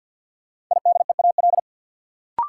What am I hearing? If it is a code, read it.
Morse code, heard as a keyed tone, about 52 wpm, 700 Hz. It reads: IDEAL
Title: Sentences from Top 1400 Words - New Word Drill - 50wpm